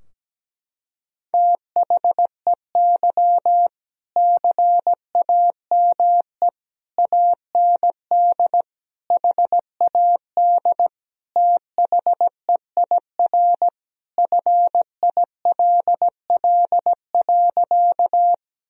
Morse code, heard as a keyed tone, about 17 wpm, 700 Hz. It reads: THEY CAME AND HAD THEIR FILL.
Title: Sentences from Top 1000 Words - Review Part 3 - 17wpm